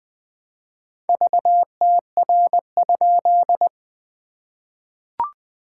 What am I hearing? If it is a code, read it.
Morse code, heard as a keyed tone, about 20 wpm, 700 Hz. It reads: VTR?
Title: SOTA QSO Practice 20wpm